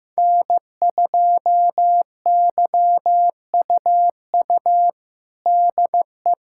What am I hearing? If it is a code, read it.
Morse code, heard as a keyed tone, about 15 wpm, 700 Hz. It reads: N2YUU DE
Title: USA Callsign DE Callsign - 5 Characters - 15wpm